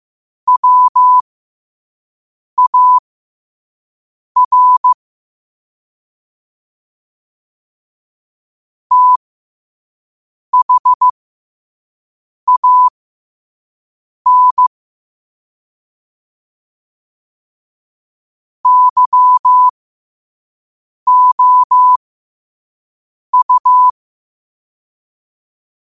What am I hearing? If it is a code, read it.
Morse code, heard as a keyed tone, about 15 wpm, spaced as for 5 wpm, 1000 Hz. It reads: WAR THAN YOU
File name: qotdfiles / 5wpm